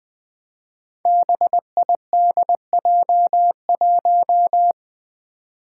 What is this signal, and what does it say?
Morse code, heard as a keyed tone, about 20 wpm, 700 Hz. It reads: BIDJ1
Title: Five Challenging Letter-Number Combinations 20wpm